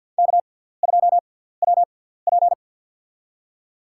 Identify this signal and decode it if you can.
Morse code, heard as a keyed tone, about 53 wpm, spaced as for 17 wpm, 700 Hz. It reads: K2WP
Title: USA Call Signs - 6x Character Spacing - 50wpm